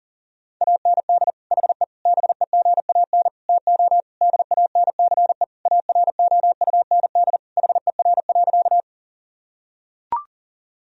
Morse code, heard as a keyed tone, about 40 wpm, 700 Hz. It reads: AND HE BEGAN TO DANCE AROUND HER.
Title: Sentences from Top 1000 Words - Review Part 4 - 40wpm